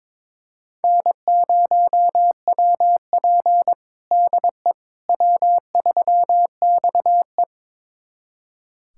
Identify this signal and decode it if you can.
Morse code, heard as a keyed tone, about 21 wpm, 700 Hz. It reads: N0WP DE W3XE
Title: USA Callsign DE Callsign - 4 Characters - 22wpm